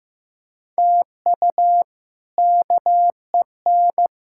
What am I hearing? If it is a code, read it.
Morse code, heard as a keyed tone, about 15 wpm, 700 Hz. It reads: TU KEN
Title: K1USN SST QSO Practice - 15wpm